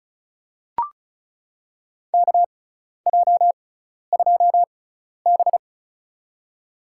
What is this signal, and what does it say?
Morse code, heard as a keyed tone, about 34 wpm, spaced as for 12 wpm, 700 Hz. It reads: KJ2B
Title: USA Call Signs - 4 Characters - 6x Character Spacing - 35wpm